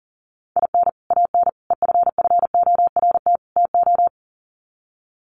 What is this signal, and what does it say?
Morse code, heard as a keyed tone, about 40 wpm, 700 Hz. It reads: IN AN EFFORT TO
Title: Sets of 4 Words 40wpm